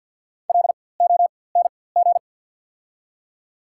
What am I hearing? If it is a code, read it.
Morse code, heard as a keyed tone, about 49 wpm, spaced as for 23 wpm, 700 Hz. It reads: LONG
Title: 4x Character Spacing - Top 100 Words - 50wpm